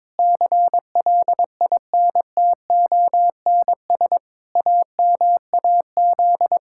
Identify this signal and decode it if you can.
Morse code, heard as a keyed tone, about 21 wpm, 700 Hz. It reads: CLINTONS AMAZ
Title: Pangram Challenge v2 - 22wpm